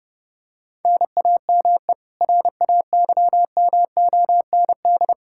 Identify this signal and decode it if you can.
Morse code, heard as a keyed tone, about 29 wpm, 700 Hz. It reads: NAME RAYMOND
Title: Practice QSO - Exchange 1 - Beginning - 30wpm